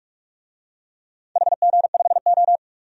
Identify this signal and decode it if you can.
Morse code, heard as a keyed tone, about 42 wpm, 700 Hz. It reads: HG5O
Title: DXCC-100 Call Signs - 4 Characters - Repeated 2x - 45wpm